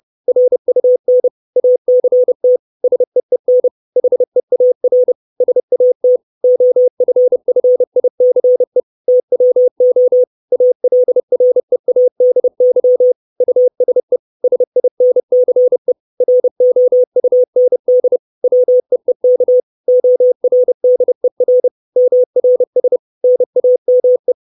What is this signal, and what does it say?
Morse code, heard as a keyed tone, about 30 wpm, 500 Hz. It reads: RUN ACT SEEN HEAR SAT OFFICE TWO ALREADY USE SINCE ROUND WEEK ORDER MRS NAME
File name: 632 Words 30WPM